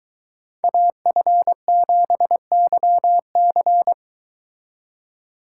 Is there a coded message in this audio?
Yes: Morse code from 0.6 to 3.9 s, heard as a keyed tone, about 22 wpm, 700 Hz. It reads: AF7YC